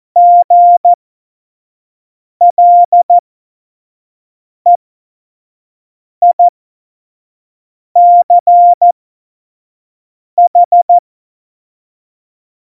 Morse code, heard as a keyed tone, about 14 wpm, spaced as for 5 wpm, 700 Hz. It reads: GLEICH
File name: Morsekurs / Witze / Witz 27 Juni 17, Tempo 25 F70